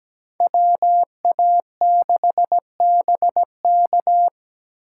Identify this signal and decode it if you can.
Morse code, heard as a keyed tone, about 21 wpm, 700 Hz. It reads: WA6BK
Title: Speed Racing - USA 2x2 Call Signs 17wpm